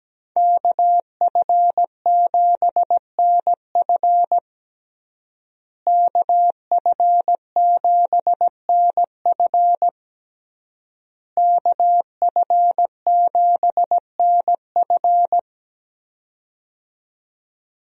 Morse code, heard as a keyed tone, about 17 wpm, 700 Hz. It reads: KF7NF KF7NF KF7NF
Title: USA Call Signs - 5 Characters - Repeated 3x - 17wpm